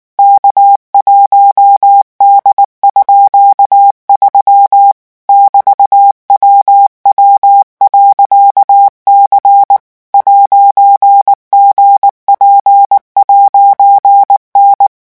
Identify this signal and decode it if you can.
Morse code, heard as a keyed tone, about 19 wpm, 800 Hz. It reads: K1D_3 <BT>WW.C 'GP'N
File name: morse mp3 / revision14-vitesse 18 mots